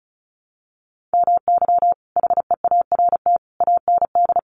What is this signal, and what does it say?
Morse code, heard as a keyed tone, about 34 wpm, 700 Hz. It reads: MY HEART AND
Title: Sentences from Top 1200 Words 35wpm